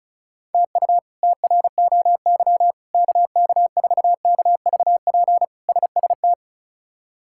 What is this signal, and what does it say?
Morse code, heard as a keyed tone, about 36 wpm, 700 Hz. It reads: TU TROY KK4KVP SST
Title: K1USN SST QSO Practice - 35wpm